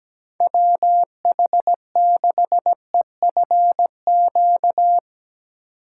Morse code, heard as a keyed tone, about 17 wpm, 700 Hz. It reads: WH6EFQ
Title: USA Call Signs - 6 Characters - 6x Character Spacing - 17wpm